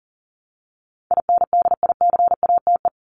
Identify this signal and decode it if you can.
Morse code, heard as a keyed tone, about 40 wpm, 700 Hz. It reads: INDICATE
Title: ED Suffix - Top 100 Words with Base Words 40wpm